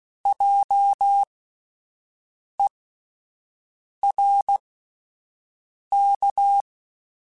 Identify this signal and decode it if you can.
Morse code, heard as a keyed tone, about 16 wpm, spaced as for 5 wpm, 800 Hz. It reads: JERK
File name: Spaced Alphas / Spaced Alphas mp3s / Js sp @10